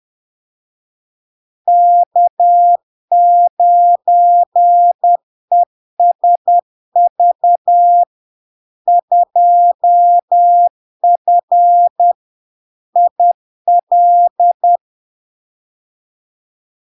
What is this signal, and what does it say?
Morse code, heard as a keyed tone, about 10 wpm, 700 Hz. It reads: K9ESV 2F IL
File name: LARK-FD-2022-Log-in-CW-10wpm-700-Hz